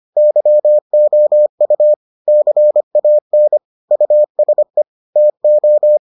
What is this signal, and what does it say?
Morse code, heard as a keyed tone, about 25 wpm, 600 Hz. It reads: YOU CAN USE TO